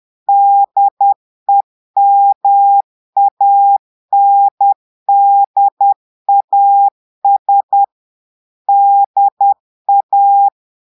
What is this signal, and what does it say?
Morse code, heard as a keyed tone, about 10 wpm, 800 Hz. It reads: DEMANDAS DA
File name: portugais texte en clair 2 vitesse 10 mots minutes